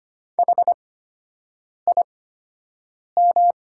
Morse code, heard as a keyed tone, about 25 wpm, spaced as for 7 wpm, 700 Hz. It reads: HIM
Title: 8x Character Spacing - Top 100 Words - 25wpm